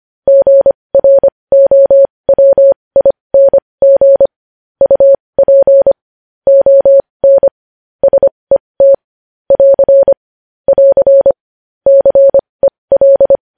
Morse code, heard as a keyed tone, about 25 wpm, spaced as for 20 wpm, 550 Hz. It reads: GROWING UP ON SET <AR> <AR> CEL
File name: podcast / News headlines-1x-25x20